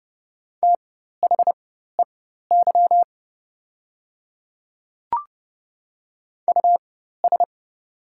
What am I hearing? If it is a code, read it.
Morse code, heard as a keyed tone, about 29 wpm, spaced as for 14 wpm, 700 Hz. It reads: THEY US
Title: 4x Character Spacing - Top 500 Words - 30wpm